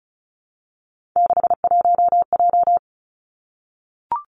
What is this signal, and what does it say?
Morse code, heard as a keyed tone, about 36 wpm, 700 Hz. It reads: 61J
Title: Three Challenging Letter-Number Combinations 35wpm